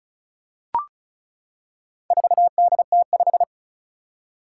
Morse code, heard as a keyed tone, about 36 wpm, 690 Hz. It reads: 4DT5
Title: Four Challenging Letter-Number Combinations 35wpm